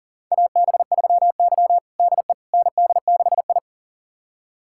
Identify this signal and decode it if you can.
Morse code, heard as a keyed tone, about 40 wpm, 700 Hz. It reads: AB3Y DE ND6I
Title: USA Callsign DE Callsign - 4 Characters - 40wpm